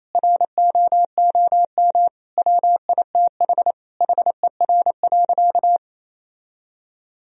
Morse code, heard as a keyed tone, about 28 wpm, 700 Hz. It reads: ROOM WITH HER.
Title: Sentences from Top 1000 Words - Review Part 2 - 28wpm